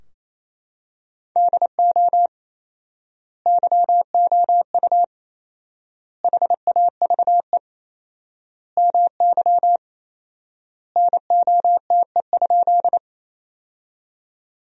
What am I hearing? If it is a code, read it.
Morse code, heard as a keyed tone, about 28 wpm, 700 Hz. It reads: DO YOU HAVE MY NOTE?
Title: 4x Word-Spacing - Sentences from Top 500 Words - 28wpm